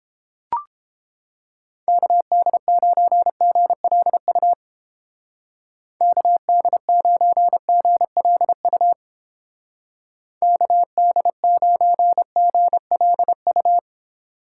Morse code, heard as a keyed tone, about 29 wpm, 700 Hz. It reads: KD9GLU KD9GLU KD9GLU